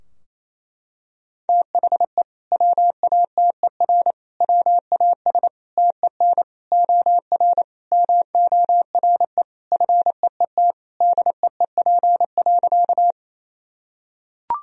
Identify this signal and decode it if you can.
Morse code, heard as a keyed tone, about 28 wpm, 700 Hz. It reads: THE WATER WAS TEN OR MORE FEET DEEP.
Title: Sentences from Top 1000 Words - Review Part 3 - 28wpm